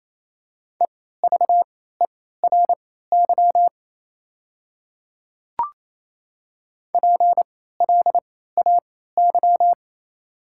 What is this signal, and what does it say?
Morse code, heard as a keyed tone, about 28 wpm, spaced as for 16 wpm, 700 Hz. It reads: EVERY PLAY
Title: 3x Character Spacing - Top 200 Words - 28wpm